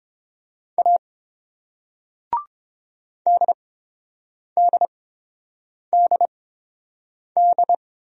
Morse code, heard as a keyed tone, about 27 wpm, spaced as for 7 wpm, 700 Hz. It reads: A DDDD
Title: Speed-Racing Single Letters 22wpm